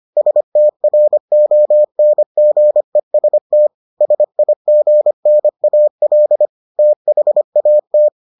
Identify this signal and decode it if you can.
Morse code, heard as a keyed tone, about 25 wpm, 600 Hz. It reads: STRONGEST SIGNAL THAT